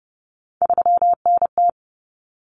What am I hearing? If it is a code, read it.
Morse code, heard as a keyed tone, about 30 wpm, 700 Hz. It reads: 3NT
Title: Common Cut Numbers - 3-Digit 30wpm